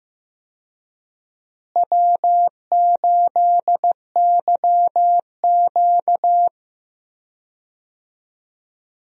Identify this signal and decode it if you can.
Morse code, heard as a keyed tone, about 15 wpm, 700 Hz. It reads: W8YQ
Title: USA Call Signs - 4 Characters - Repeated 3x - 15wpm